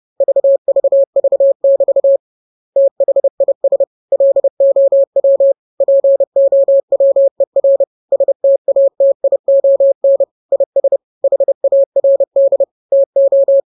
Morse code, heard as a keyed tone, about 30 wpm, 550 Hz. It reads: VVV<BT> THIS LOW POWER STATION IS HARD TO